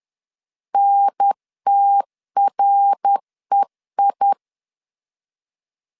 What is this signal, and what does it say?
Morse code, heard as a keyed tone, about 10 wpm, 800 Hz. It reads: NTREI
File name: CW / DET / 1 3 1500C S C 10wpm 3335